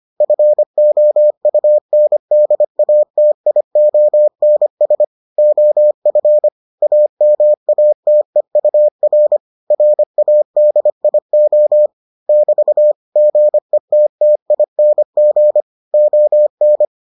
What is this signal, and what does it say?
Morse code, heard as a keyed tone, about 25 wpm, 600 Hz. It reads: FOUNDATIONS OF AMATEUR RADIO <BT> GETTING ON